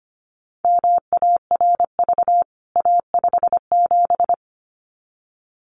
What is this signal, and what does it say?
Morse code, heard as a keyed tone, about 25 wpm, 700 Hz. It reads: MARV A57
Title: CWT Exchange 25wpm